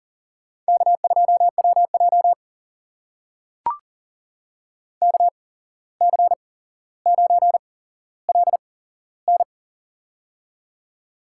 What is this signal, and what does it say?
Morse code, heard as a keyed tone, about 40 wpm, spaced as for 16 wpm, 700 Hz. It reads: K2WJ KC9LN
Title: USA Call Signs - 8x Character Spacing - 40wpm